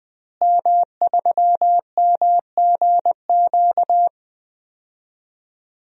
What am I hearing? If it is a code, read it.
Morse code, heard as a keyed tone, about 20 wpm, 700 Hz. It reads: M3MGQ
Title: DXCC-100 Call Signs - 5 Characters - Repeated 3x - 20wpm